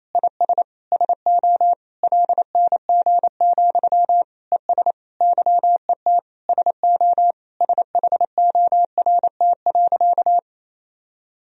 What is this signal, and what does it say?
Morse code, heard as a keyed tone, about 28 wpm, 700 Hz. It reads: IS SO LNG, ES YET SO SHORT.
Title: Sentences from Top 500 Words and Abbreviations 28wpm